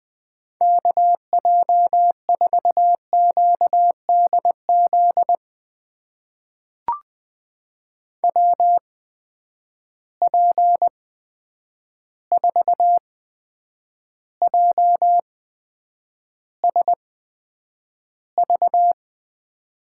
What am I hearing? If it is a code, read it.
Morse code, heard as a keyed tone, about 20 wpm, spaced as for 9 wpm, 700 Hz. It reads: KJ4QDZ WP4JSV
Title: USA Call Signs - 6 Characters - 8x Character Spacing - 20wpm